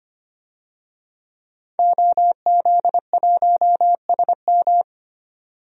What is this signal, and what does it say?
Morse code, heard as a keyed tone, about 25 wpm, 700 Hz. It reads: OZ1SM
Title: DXCC-100 Call Signs - 5 Characters - 25wpm